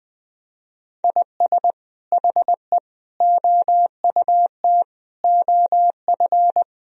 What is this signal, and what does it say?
Morse code, heard as a keyed tone, about 21 wpm, 700 Hz. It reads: IS HE OUT OF
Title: Sentences from Top 1000 Words - Review Part 4 - 20wpm